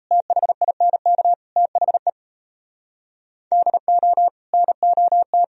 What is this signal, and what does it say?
Morse code, heard as a keyed tone, about 32 wpm, 700 Hz. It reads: THINK THE DO NOT
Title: Speed-Racing - Sets of 4 Words 28wpm